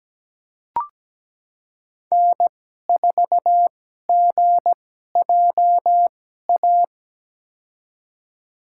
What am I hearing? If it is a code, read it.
Morse code, heard as a keyed tone, about 17 wpm, spaced as for 12 wpm, 700 Hz. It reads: N4GJA
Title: USA Call Signs - 5 Characters - 2x Character Spacing - 17wpm